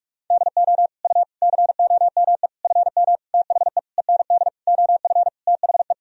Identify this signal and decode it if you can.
Morse code, heard as a keyed tone, about 46 wpm, 700 Hz. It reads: DO U COME FM THE END OF THE
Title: Sentences from Top 200 Words and Abbreviations 45wpm